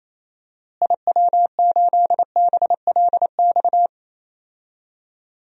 Morse code, heard as a keyed tone, about 28 wpm, 700 Hz. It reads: IW8BLX
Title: DXCC-100 Call Signs - 6 Characters - Repeated 3x - 28wpm